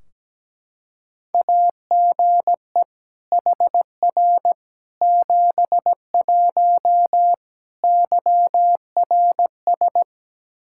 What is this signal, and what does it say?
Morse code, heard as a keyed tone, about 17 wpm, 700 Hz. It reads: AGE HR 71 YRS
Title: Practice QSO - Exchange 3 - More Info - 17wpm